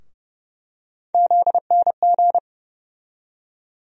Morse code, heard as a keyed tone, about 32 wpm, 700 Hz. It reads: ZNG
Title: USA Call Sign Postfixes - 3 Letters 30wpm